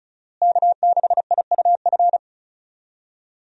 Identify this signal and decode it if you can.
Morse code, heard as a keyed tone, about 36 wpm, 700 Hz. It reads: K6IUF
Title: USA Call Signs - 5 Characters - Repeated 2x - 35wpm